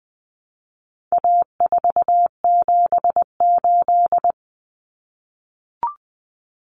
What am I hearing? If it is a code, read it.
Morse code, heard as a keyed tone, about 20 wpm, 700 Hz. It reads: A478
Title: Common Cut Numbers - 4-Digit 20wpm